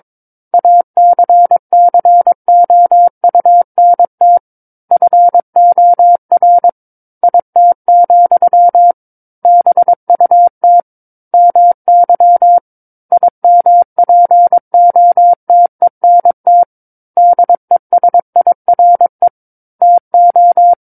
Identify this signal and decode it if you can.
Morse code, heard as a keyed tone, about 22 wpm, 700 Hz. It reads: ACCOUNT FOR IT, BUT MY IMPOTENT DESIRE TO
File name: Book One / 13-Chapter 13 - 22 WPM